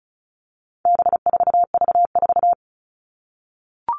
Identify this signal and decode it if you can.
Morse code, heard as a keyed tone, about 34 wpm, 700 Hz. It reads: B4V4
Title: All Cut Numbers - 4-Digit 35wpm